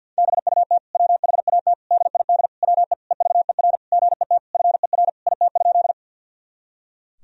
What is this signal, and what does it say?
Morse code, heard as a keyed tone, about 51 wpm, 700 Hz. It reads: BUT WHAT DID WE EVER GET FER IT?
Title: Sentences from Top 400 Words and Abbreviations 50wpm